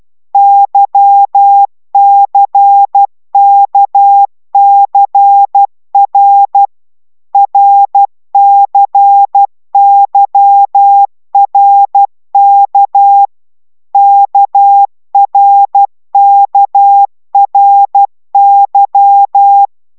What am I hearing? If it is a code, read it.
Morse code, heard as a keyed tone, about 12 wpm, 800 Hz. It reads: YCKCR RCYRK KRKRY